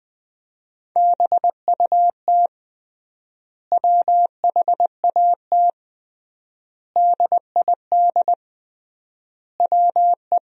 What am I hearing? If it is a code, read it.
Morse code, heard as a keyed tone, about 20 wpm, 700 Hz. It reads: BUT WHAT DID WE
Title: 3x Word-Spacing - Sentences from Top 400 Words - 20wpm